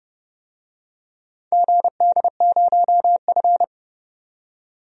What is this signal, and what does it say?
Morse code, heard as a keyed tone, about 30 wpm, 700 Hz. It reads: GD0F
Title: DXCC-100 Call Signs - 4 Characters - 30wpm